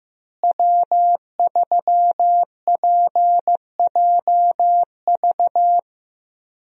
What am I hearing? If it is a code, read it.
Morse code, heard as a keyed tone, about 16 wpm, 700 Hz. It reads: W3PJV
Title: USA Call Signs - 5 Characters - Repeated 3x - 15wpm